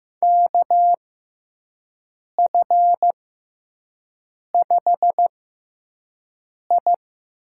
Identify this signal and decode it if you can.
Morse code, heard as a keyed tone, about 15 wpm, spaced as for 5 wpm, 700 Hz. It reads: KF5I
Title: USA Call Signs - 6x Character Spacing - 15wpm